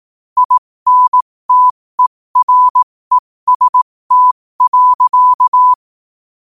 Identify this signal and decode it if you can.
Morse code, heard as a keyed tone, about 18 wpm, spaced as for 15 wpm, 1000 Hz. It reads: INTEREST.
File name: qotdfiles / 15wpm